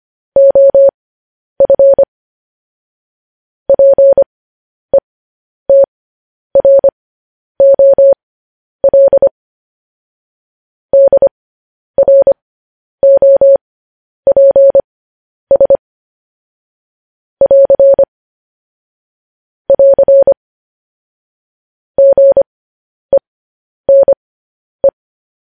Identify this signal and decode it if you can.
Morse code, heard as a keyed tone, about 25 wpm, spaced as for 10 wpm, 550 Hz. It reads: OF PETROL DROPS <AR> <AR> GENE
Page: news headlines in morse code at 10 wpm